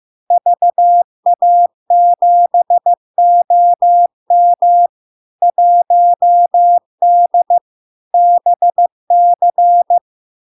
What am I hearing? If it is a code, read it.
Morse code, heard as a keyed tone, about 15 wpm, 700 Hz. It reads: VA7OM 1D BC
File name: LARK-FD-2022-Log-in-CW-15wpm-700-Hz